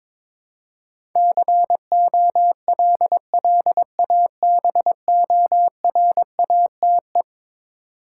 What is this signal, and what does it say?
Morse code, heard as a keyed tone, about 22 wpm, 700 Hz. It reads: COLLABORATE
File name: practice / 22wpm / Eleven Letter Words - 22wpm